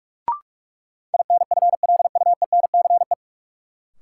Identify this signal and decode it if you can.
Morse code, heard as a keyed tone, about 46 wpm, 700 Hz. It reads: INFLUENCE
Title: Sentences from Top 1100 Words - New Word Drill Alt - 45wpm